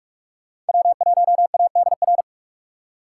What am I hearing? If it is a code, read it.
Morse code, heard as a keyed tone, about 56 wpm, 700 Hz. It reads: W1ADR
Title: Speed Racing - USA 1x3 Call Signs 45wpm